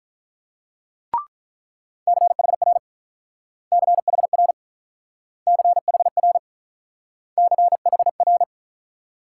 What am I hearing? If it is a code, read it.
Morse code, heard as a keyed tone, about 43 wpm, 690 Hz. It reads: CHR CHR CHR CHR
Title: Speed-Racing Multi-letter Phonemes - 35wpm